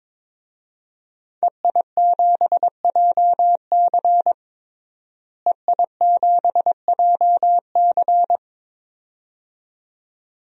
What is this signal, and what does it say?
Morse code, heard as a keyed tone, about 21 wpm, 700 Hz. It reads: EI7JC EI7JC
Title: DXCC-100 Call Signs - 5 Characters - Repeated 2x - 22wpm